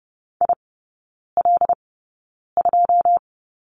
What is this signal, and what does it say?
Morse code, heard as a keyed tone, about 30 wpm, spaced as for 9 wpm, 700 Hz. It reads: IL2
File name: Lesson 19e - Letter w - Three Characters 30wpm